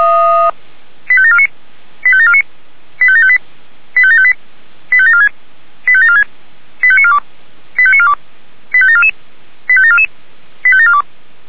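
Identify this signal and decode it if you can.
Five-tone selective call (ZVEI), heard as quick runs of five tones, tones of about 70 ms. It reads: 85749 85749 85758 85758 85746 85746 85942 85942 85744 85744 85742
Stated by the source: ZVEI five-tone selective call received with an SDR.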